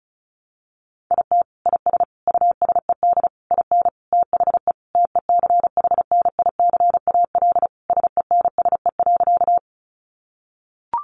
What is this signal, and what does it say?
Morse code, heard as a keyed tone, about 34 wpm, 700 Hz. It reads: IT IS USED IN THE TECHNICAL SENSE.
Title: Sentences from Top 1100 Words 35wpm